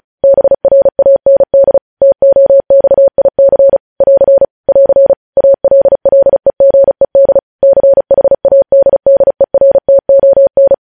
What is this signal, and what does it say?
Morse code, heard as a keyed tone, about 36 wpm, 550 Hz. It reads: BRAND TOXIC <AR> <AR> ALLEGED CHADDERTON
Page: News headlines in Morse code at 35 wpm